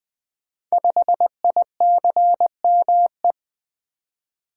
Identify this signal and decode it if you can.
Morse code, heard as a keyed tone, about 20 wpm, 700 Hz. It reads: 5ICME
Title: Code Groups - 5 Letter:Numbers - 20wpm